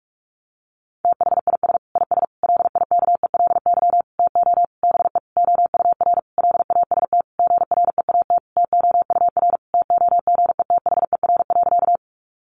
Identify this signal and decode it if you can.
Morse code, heard as a keyed tone, about 44 wpm, 700 Hz. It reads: THIS IS LIKELY TO BE OUR LAST GREAT TOUR TOGETHER.
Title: Sentences from Top 1400 Words 45wpm